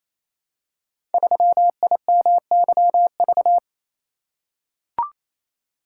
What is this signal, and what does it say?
Morse code, heard as a keyed tone, about 27 wpm, 700 Hz. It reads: 3IMYV
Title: Code Groups - 5 Letter:Numbers - 28wpm